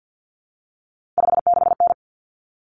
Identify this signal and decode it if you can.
Morse code, heard as a keyed tone, about 50 wpm, 700 Hz. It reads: 56N
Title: Common Cut Numbers - 3-Digit 50wpm